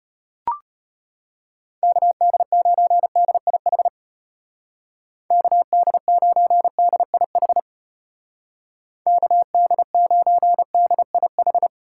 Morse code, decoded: KD9DIH KD9DIH KD9DIH

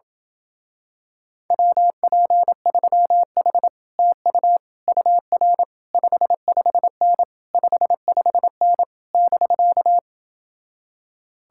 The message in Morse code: WP3H TU UR 55N 55N <BK>

